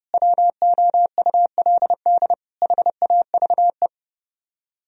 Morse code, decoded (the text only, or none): WOULD HAVE